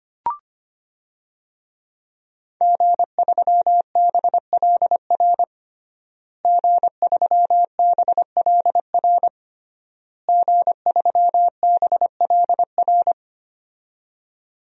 Morse code, heard as a keyed tone, about 25 wpm, 700 Hz. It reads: G3BLR G3BLR G3BLR
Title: DXCC-100 Call Signs - 5 Characters - Repeated 3x - 25wpm